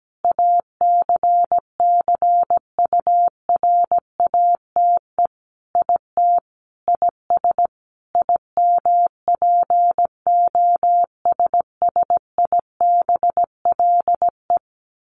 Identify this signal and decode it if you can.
Morse code, heard as a keyed tone, about 17 wpm, 700 Hz. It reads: ACCURATE IT IS IMPOSSIBLE
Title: Sentences from Top 1200 Words 17wpm